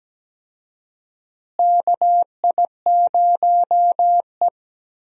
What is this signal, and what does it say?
Morse code, heard as a keyed tone, about 17 wpm, 700 Hz. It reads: KI0E
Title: USA Call Signs - 4 Characters - Repeated 3x - 17wpm